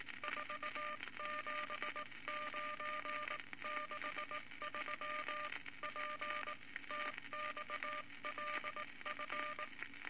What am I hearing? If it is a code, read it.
Morse code, heard as a keyed tone, about 19 wpm, 1200 Hz, with noise. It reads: 47963 PTXLF
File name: Morse Code Records / 19 B LCM 167 Twelve Words Per Minute